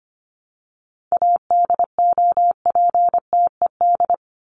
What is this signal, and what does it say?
Morse code, heard as a keyed tone, about 25 wpm, 700 Hz. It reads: ADOPTED
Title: ED Suffix - Top 100 Words with Base Words 25wpm